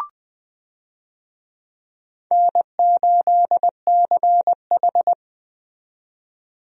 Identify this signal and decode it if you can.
Morse code, heard as a keyed tone, about 20 wpm, 700 Hz. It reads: N8CH
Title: USA Call Signs - 4 Characters - 20wpm